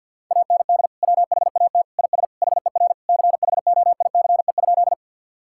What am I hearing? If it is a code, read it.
Morse code, heard as a keyed tone, about 52 wpm, 700 Hz. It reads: AND WHAT IS HER CHOICE?
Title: Sentences from Top 1000 Words - Review Part 6 - 50wpm